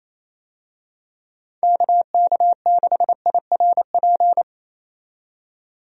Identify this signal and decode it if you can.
Morse code, heard as a keyed tone, about 28 wpm, 700 Hz. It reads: KK6IRP
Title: USA Call Signs - 6 Characters - 28wpm